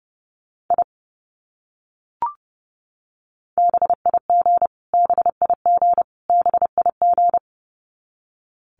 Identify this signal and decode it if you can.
Morse code, heard as a keyed tone, about 28 wpm, 700 Hz. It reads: I BIG BIG BIG
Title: Top 300 Words Repeated 30wpm